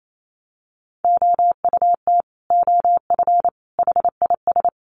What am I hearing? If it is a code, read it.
Morse code, heard as a keyed tone, about 28 wpm, 700 Hz. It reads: OUT OF HIS
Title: Sets of 3 Words 28wpm